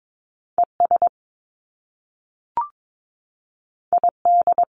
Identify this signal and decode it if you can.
Morse code, heard as a keyed tone, about 22 wpm, 700 Hz. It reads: ES ID
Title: Top Two Letter Combinations in Words 22wpm